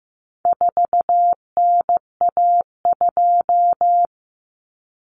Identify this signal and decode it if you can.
Morse code, heard as a keyed tone, about 15 wpm, 700 Hz. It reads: 4NA2
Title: Common Cut Numbers - 4-Digit 15wpm